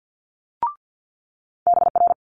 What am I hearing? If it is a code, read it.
Morse code, heard as a keyed tone, about 50 wpm, 700 Hz. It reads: BR
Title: Multi-letter Phonemes Quick Fire 50wpm